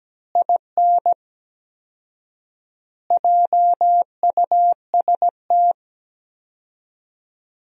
Morse code, heard as a keyed tone, about 17 wpm, 700 Hz. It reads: IN JUST
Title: 4x Word-Spacing - Sentences from Top 300 Words - 17wpm